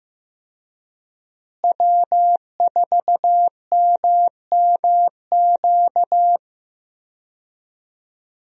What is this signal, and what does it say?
Morse code, heard as a keyed tone, about 15 wpm, 700 Hz. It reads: W4MMQ
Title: USA Call Signs - 5 Characters - Repeated 3x - 15wpm